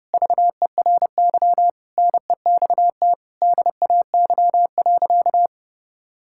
Morse code, heard as a keyed tone, about 32 wpm, 700 Hz. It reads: VERY NEXT DAY.
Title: Sentences from Top 1000 Words - Review Part 6 - 30wpm